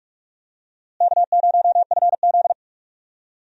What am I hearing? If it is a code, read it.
Morse code, heard as a keyed tone, about 46 wpm, 700 Hz. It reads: K0FZ
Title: USA Call Signs - 4 Characters - Repeated 2x - 45wpm